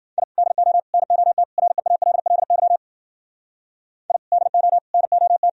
Morse code, heard as a keyed tone, about 52 wpm, 700 Hz. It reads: I DO NOT REALLY I DO NOT